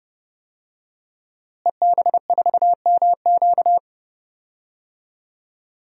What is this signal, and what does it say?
Morse code, heard as a keyed tone, about 30 wpm, 700 Hz. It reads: EB4MQ
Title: DXCC-100 Call Signs - 5 Characters - Repeated 3x - 30wpm